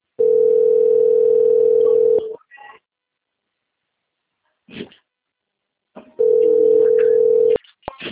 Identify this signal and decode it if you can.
Ringback tone (North America), 440 + 480 Hz.
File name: out- - - - - .39781